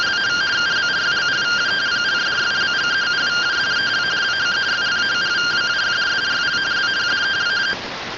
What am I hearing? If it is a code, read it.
Two-tone teleprinter signal, about 170 Hz shift.